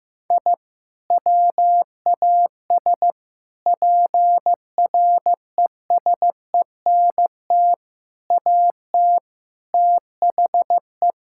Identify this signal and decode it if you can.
Morse code, heard as a keyed tone, about 15 wpm, 700 Hz. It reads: I WAS PRESENT AT THE
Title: Sentences from Top 1000 Words - Review Part 5 - 15wpm